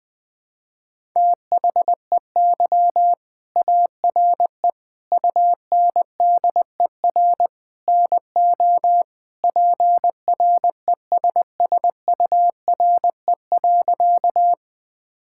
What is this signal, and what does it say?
Morse code, heard as a keyed tone, about 19 wpm, 700 Hz. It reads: THEY ARE UNDER NO PRESSURE.
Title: Sentences from Top 1000 Words - Review Part 6 - 20wpm